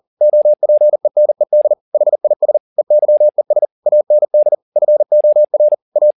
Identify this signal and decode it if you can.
Morse code, heard as a keyed tone, about 40 wpm, 600 Hz. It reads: OPENED HIS EYES AND FOR A